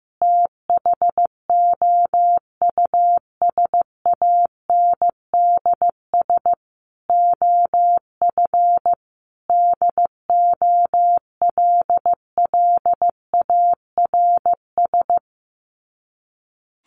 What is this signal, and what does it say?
Morse code, heard as a keyed tone, about 15 wpm, 700 Hz. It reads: THOUSANDS OF DOLLARS
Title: Sets of 5 Words Encore 15wpm